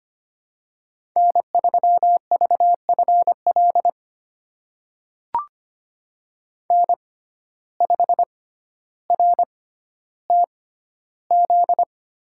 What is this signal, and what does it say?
Morse code, heard as a keyed tone, about 25 wpm, spaced as for 13 wpm, 700 Hz. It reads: N3VFL N5RTZ
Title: USA Call Signs - 5 Characters - 6x Character Spacing - 25wpm